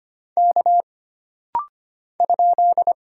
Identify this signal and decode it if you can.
Morse code, heard as a keyed tone, about 25 wpm, 700 Hz. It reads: K ?
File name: Single Character (Letter:Number:Punctuation:Prosign) - Rapid-Fire - 25wpm